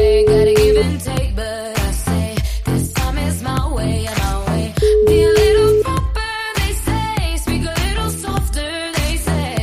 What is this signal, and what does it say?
Ringback tone (Europe, 425 Hz), with noise.